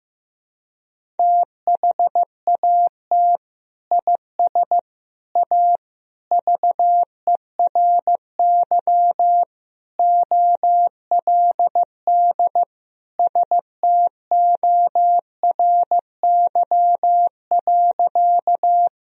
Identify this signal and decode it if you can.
Morse code, heard as a keyed tone, about 15 wpm, 700 Hz. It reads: THAT IS A VERY OLD STORY.
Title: Sentences from Top 1000 Words - Review Part 1 - 15wpm